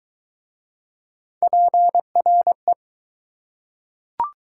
Morse code, heard as a keyed tone, about 23 wpm, 700 Hz. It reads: PRE